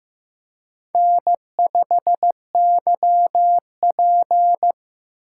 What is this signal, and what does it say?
Morse code, heard as a keyed tone, about 18 wpm, 700 Hz. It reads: N5YP
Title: Speed Racing - USA 1x2 Call Signs 15wpm